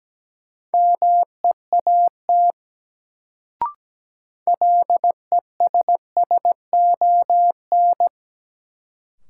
Morse code, heard as a keyed tone, about 17 wpm, 700 Hz. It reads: MEAT LESSON
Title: Sentences from Top 1200 Words - New Word Drill Alt - 17wpm